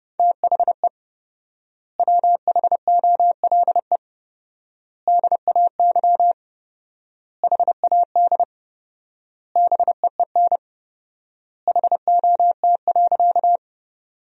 Morse code, heard as a keyed tone, about 30 wpm, 700 Hz. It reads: THE WHOLE DAY HAD BEEN HOT.
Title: 4x Word-Spacing - Sentences from Top 500 Words - 30wpm